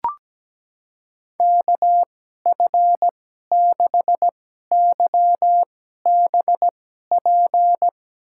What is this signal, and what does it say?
Morse code, heard as a keyed tone, about 17 wpm, spaced as for 12 wpm, 700 Hz. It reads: KF6YBP
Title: USA Call Signs - 6 Characters - 2x Character Spacing - 17wpm